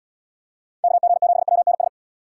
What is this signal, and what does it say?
Morse code, heard as a keyed tone, about 69 wpm, 700 Hz. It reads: KG7WNS